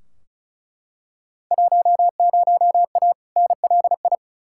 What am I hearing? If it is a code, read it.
Morse code, heard as a keyed tone, about 36 wpm, 690 Hz. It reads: 10A NLI
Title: Field Day Exchanges Encore - 35wpm